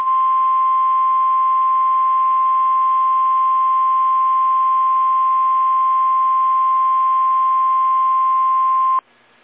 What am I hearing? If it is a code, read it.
Emergency Alert System (SAME) tones (the 1050 Hz attention tone). It is heard from 0.6 to 8.9 s.